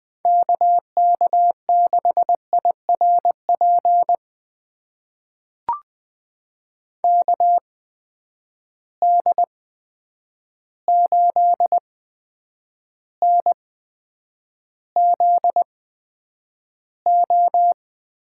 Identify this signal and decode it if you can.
Morse code, heard as a keyed tone, about 20 wpm, spaced as for 9 wpm, 700 Hz. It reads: KK6IRP KD8NZO
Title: USA Call Signs - 6 Characters - 8x Character Spacing - 20wpm